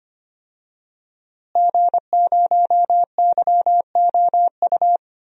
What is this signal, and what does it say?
Morse code, heard as a keyed tone, about 25 wpm, 700 Hz. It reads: G0YOU
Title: DXCC-100 Call Signs - 5 Characters - Repeated 2x - 25wpm